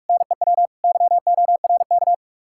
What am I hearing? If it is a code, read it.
Morse code, heard as a keyed tone, about 58 wpm, 700 Hz. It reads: NEW YORK